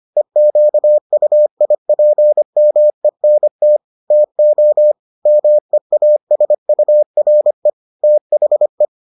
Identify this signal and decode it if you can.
Morse code, heard as a keyed tone, about 25 wpm, 600 Hz. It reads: EQUIPMENT TO MEASURE THE